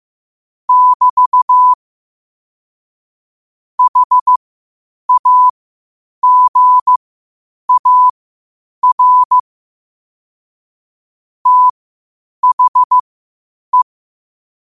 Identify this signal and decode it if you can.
Morse code, heard as a keyed tone, about 15 wpm, spaced as for 8 wpm, 1000 Hz. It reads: <BT> HAGAR THE